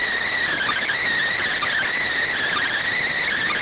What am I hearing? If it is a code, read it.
Emergency Alert System (SAME) tones, heard as harsh buzzing bursts. It is heard from 0.5 to 3.3 s.